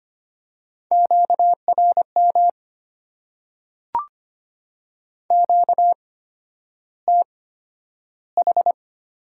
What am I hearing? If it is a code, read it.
Morse code, heard as a keyed tone, about 25 wpm, spaced as for 11 wpm, 690 Hz. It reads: QRM QTH
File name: Top 90 QSO Elements - 8x Character Spacing - 25wpm